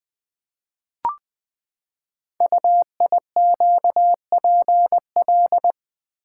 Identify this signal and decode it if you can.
Morse code, heard as a keyed tone, about 20 wpm, 700 Hz. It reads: UIQPL
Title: Code Groups - 5 Letters - 20wpm